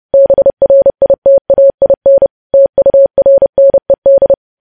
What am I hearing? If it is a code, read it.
Morse code, heard as a keyed tone, about 30 wpm, 550 Hz. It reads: BRITAIN TURNED
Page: news headlines in morse code at 30 wpm